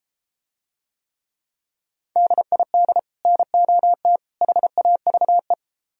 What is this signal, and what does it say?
Morse code, heard as a keyed tone, about 34 wpm, 700 Hz. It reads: DID NOT HAVE